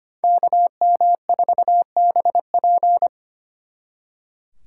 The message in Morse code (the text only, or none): KM4BP